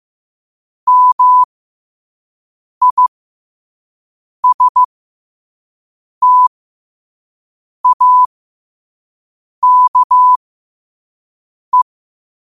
Morse code, heard as a keyed tone, about 15 wpm, spaced as for 5 wpm, 1000 Hz. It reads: MISTAKE
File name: qotdfiles / 5wpm